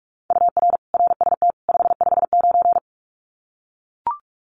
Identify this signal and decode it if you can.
Morse code, heard as a keyed tone, about 44 wpm, 700 Hz. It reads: UR RST 559
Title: Common QSO Phrases 45wpm